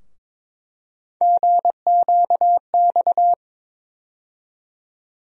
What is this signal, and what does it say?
Morse code, heard as a keyed tone, about 22 wpm, 700 Hz. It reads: GQX
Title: USA Call Sign Postfixes - 3 Letters 22wpm